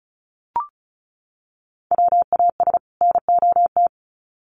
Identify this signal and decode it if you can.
Morse code, heard as a keyed tone, about 34 wpm, 700 Hz. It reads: WAS NOT
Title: Sets of 2 Words 35wpm